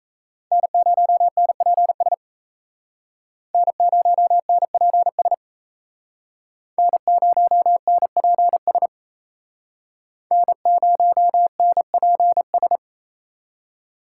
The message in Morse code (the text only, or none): N0NPS N0NPS N0NPS N0NPS